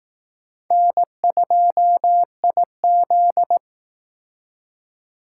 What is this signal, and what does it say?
Morse code, heard as a keyed tone, about 18 wpm, 700 Hz. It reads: N2IZ
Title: Speed Racing - USA 1x2 Call Signs 15wpm